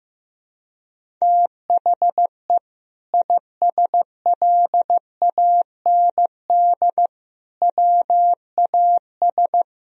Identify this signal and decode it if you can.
Morse code, heard as a keyed tone, about 15 wpm, 700 Hz. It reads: THE ISLAND WAS
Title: Sentences from Top 500 Words and Abbreviations 15wpm